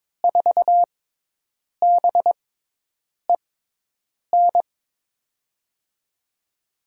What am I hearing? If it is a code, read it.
Morse code, heard as a keyed tone, about 22 wpm, spaced as for 8 wpm, 700 Hz. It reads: 4BEN
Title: USA Call Signs - 5 Characters - 6x Character Spacing - 22wpm